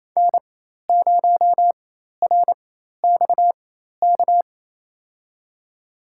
Morse code, heard as a keyed tone, about 28 wpm, spaced as for 13 wpm, 700 Hz. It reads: N0RXK
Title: USA Call Signs - 5 Characters - 4x Character Spacing - 28wpm